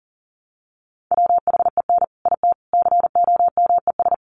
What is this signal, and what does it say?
Morse code, heard as a keyed tone, about 40 wpm, 700 Hz. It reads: WHEN IT COMES
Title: Sets of 3 Words 40wpm